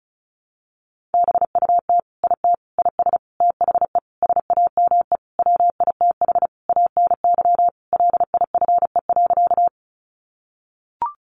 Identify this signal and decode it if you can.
Morse code, heard as a keyed tone, about 34 wpm, 700 Hz. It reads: BUT IT IS THE SAME WITH ANY LIFE.
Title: Sentences from Top 300 Words 35wpm